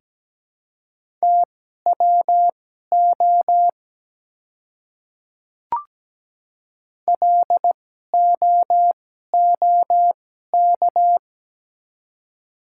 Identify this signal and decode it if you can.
Morse code, heard as a keyed tone, about 17 wpm, spaced as for 12 wpm, 700 Hz. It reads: TWO LOOK
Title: 2x Character Spacing - Top 200 Words - 17wpm